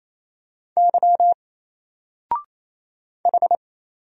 Morse code, heard as a keyed tone, about 27 wpm, 700 Hz. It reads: Y H
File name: Single-Letter-Number-28wpm